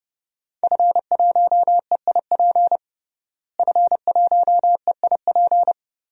Morse code, heard as a keyed tone, about 30 wpm, 700 Hz. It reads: F1EIP F1EIP
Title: DXCC-100 Call Signs - 5 Characters - Repeated 3x - 30wpm